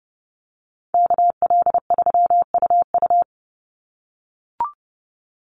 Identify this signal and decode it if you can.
Morse code, heard as a keyed tone, about 29 wpm, 700 Hz. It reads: KL3UU
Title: USA Call Signs 30wpm